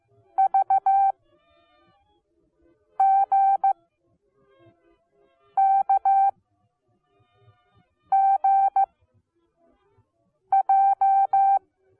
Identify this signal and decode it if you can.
Morse code, heard as a keyed tone, about 15 wpm, spaced as for 4 wpm, 780 Hz. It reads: VGKGJ